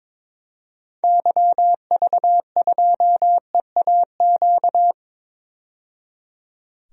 Morse code, heard as a keyed tone, about 22 wpm, 700 Hz. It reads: YV2EAQ
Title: DXCC-100 Call Signs - 6 Characters - 22wpm